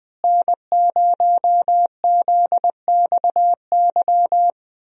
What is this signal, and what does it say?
Morse code, heard as a keyed tone, about 20 wpm, 700 Hz. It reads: N0ZXY
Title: USA Call Signs - 5 Characters - Repeated 3x - 20wpm